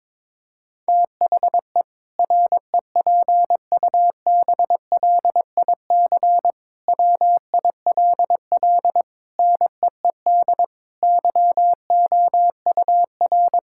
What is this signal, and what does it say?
Morse code, heard as a keyed tone, about 22 wpm, 700 Hz. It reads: THE REPUBLIC WILL NEED YOUR
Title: Sentences from Top 1600 Words 22wpm